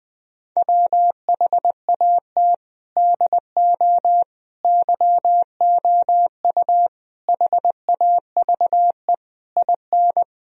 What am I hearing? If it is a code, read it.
Morse code, heard as a keyed tone, about 21 wpm, 700 Hz. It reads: WHAT DO YOU HAVE IN
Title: Sentences from Top 1000 Words - Review Part 2 - 20wpm